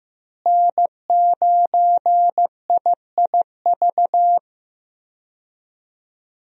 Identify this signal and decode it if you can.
Morse code, heard as a keyed tone, about 15 wpm, 700 Hz. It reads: N9IIV
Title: USA Call Signs - 5 Characters - Repeated 2x - 15wpm